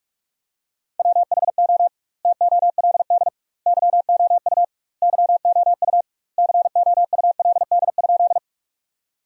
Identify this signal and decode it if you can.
Morse code, heard as a keyed tone, about 41 wpm, 700 Hz. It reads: WHO TOLD YOU YOU COULD?
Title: 2x Word-Spacing - Sentences from Top 400 Words - 45wpm